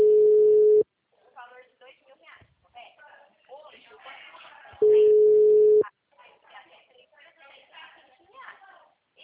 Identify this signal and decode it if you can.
Ringback tone (Europe, 425 Hz), with speech.